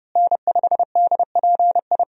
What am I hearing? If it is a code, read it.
Morse code, heard as a keyed tone, about 30 wpm, 700 Hz. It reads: N5DPI
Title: USA Call Signs - 5 Characters - Repeated 3x - 30wpm